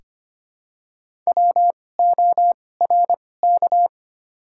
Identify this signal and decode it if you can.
Morse code, heard as a keyed tone, about 25 wpm, spaced as for 18 wpm, 700 Hz. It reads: WORK
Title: 2x Character Spacing - Top 300 Words - 25wpm